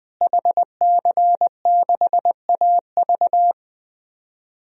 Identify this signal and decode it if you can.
Morse code, heard as a keyed tone, about 21 wpm, 700 Hz. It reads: HC6AV
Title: DXCC-100 Call Signs - 5 Characters - Repeated 2x - 20wpm